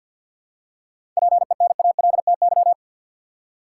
Morse code, heard as a keyed tone, about 48 wpm, 700 Hz. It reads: PENALTY
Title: Seven Letter Words - 50wpm